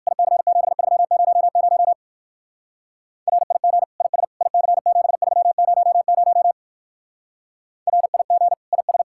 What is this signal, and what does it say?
Morse code, heard as a keyed tone, about 59 wpm, 700 Hz. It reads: IC7300 RIG IS IC7300 RIG IS